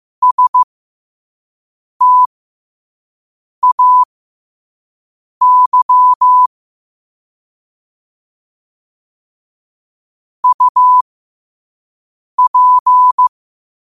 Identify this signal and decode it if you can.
Morse code, heard as a keyed tone, about 15 wpm, spaced as for 5 wpm, 1000 Hz. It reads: STAY UP
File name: qotdfiles / 5wpm